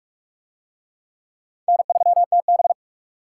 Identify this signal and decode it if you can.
Morse code, heard as a keyed tone, about 40 wpm, 700 Hz. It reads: N3TB